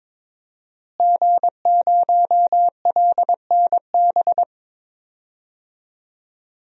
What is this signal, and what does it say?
Morse code, heard as a keyed tone, about 22 wpm, 700 Hz. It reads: G0LNB
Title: DXCC-100 Call Signs - 5 Characters - Repeated 3x - 22wpm